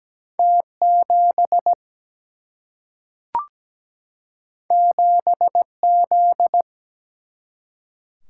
Morse code, heard as a keyed tone, about 17 wpm, 700 Hz. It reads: T7 7Z
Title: Single-Letter Single-Number 17wpm